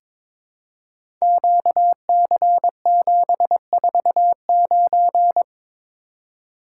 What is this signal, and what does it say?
Morse code, heard as a keyed tone, about 22 wpm, 700 Hz. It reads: QC749
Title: Code Groups - 5 Letter:Numbers - 22wpm